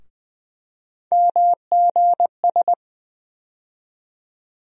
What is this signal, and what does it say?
Morse code, heard as a keyed tone, about 20 wpm, 700 Hz. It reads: MGS
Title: USA Call Sign Postfixes - 3 Letters 20wpm